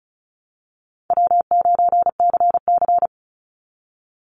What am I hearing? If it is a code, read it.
Morse code, heard as a keyed tone, about 34 wpm, 700 Hz. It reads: W9CC
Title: Easy Call Signs Repeated 2x - 35wpm